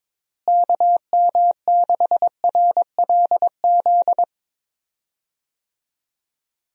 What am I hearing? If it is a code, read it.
Morse code, heard as a keyed tone, about 21 wpm, 700 Hz. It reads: KM6RLZ
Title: USA Call Signs - 6 Characters - Repeated 3x - 22wpm